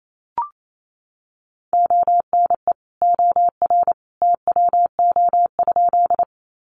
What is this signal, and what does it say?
Morse code, heard as a keyed tone, about 28 wpm, 700 Hz. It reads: ONE OR TWO?
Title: Sentences from Top 100 Words 28wpm